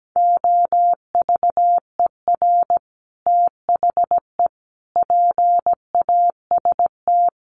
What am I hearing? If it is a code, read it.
Morse code, heard as a keyed tone, about 17 wpm, 700 Hz. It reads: OVER THE PAST
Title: Sets of 3 Words 17wpm